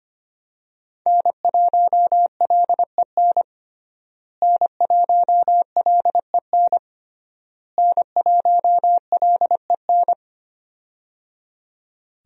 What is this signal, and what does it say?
Morse code, heard as a keyed tone, about 25 wpm, 700 Hz. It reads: N1LEN N1LEN N1LEN
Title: USA Call Signs - 5 Characters - Repeated 3x - 25wpm